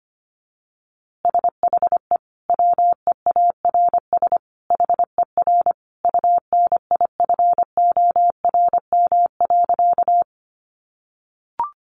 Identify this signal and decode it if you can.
Morse code, heard as a keyed tone, about 25 wpm, 700 Hz. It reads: SHE WEARS HER UNIFORM.
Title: Easy Phrases 25wpm